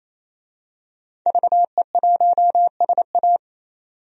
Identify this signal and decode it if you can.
Morse code, heard as a keyed tone, about 28 wpm, 700 Hz. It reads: VE1SA
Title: DXCC-100 Call Signs - 5 Characters - 28wpm